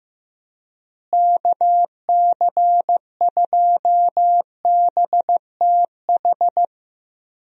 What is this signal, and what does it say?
Morse code, heard as a keyed tone, about 15 wpm, 700 Hz. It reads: KC2BTH
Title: USA Call Signs - 6 Characters - Repeated 2x - 15wpm